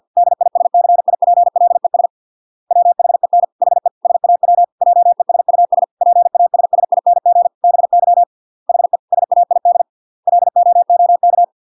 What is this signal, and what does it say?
Morse code, heard as a keyed tone, about 50 wpm, 700 Hz. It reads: DISCIPLES WHEN HE SAW JESUS PASSING BY HE SAID LOOK